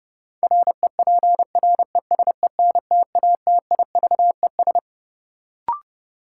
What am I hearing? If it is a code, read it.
Morse code, heard as a keyed tone, about 30 wpm, 700 Hz. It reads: REPRESENTATIVES
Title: Sets of 5 Words Encore - New Word Drill - 30wpm